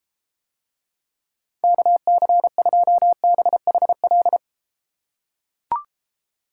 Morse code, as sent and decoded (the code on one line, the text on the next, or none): -.- -.-. ..--- -... .... .-..
KC2BHL